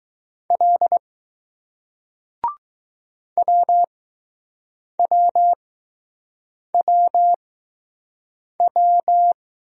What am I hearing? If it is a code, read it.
Morse code, heard as a keyed tone, about 20 wpm, 700 Hz. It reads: L W W W W